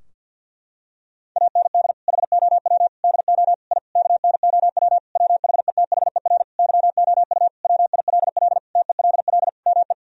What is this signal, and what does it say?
Morse code, heard as a keyed tone, about 52 wpm, 700 Hz. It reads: AND HOW DO I KNOW WHETHER YOU WILL TELL ME
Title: Sentences from Top 1000 Words - Review Part 3 - 50wpm